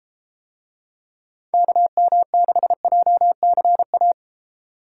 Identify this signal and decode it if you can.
Morse code, heard as a keyed tone, about 33 wpm, 700 Hz. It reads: KM6JCA